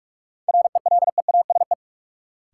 Morse code, heard as a keyed tone, about 45 wpm, 700 Hz. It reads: RELEASE
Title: Sentences from Top 1100 Words - New Word Drill Alt - 45wpm